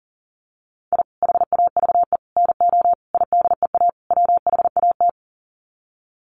Morse code, heard as a keyed tone, about 37 wpm, 700 Hz. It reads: I HAVE NO IDEA WHAT
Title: Sets of 5 Words Encore 40wpm